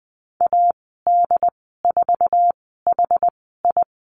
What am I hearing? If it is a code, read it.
Morse code, heard as a keyed tone, about 20 wpm, spaced as for 14 wpm, 700 Hz. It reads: AD4HI
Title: 082 Lesson 14g - Number 2 - Call Signs 20wpm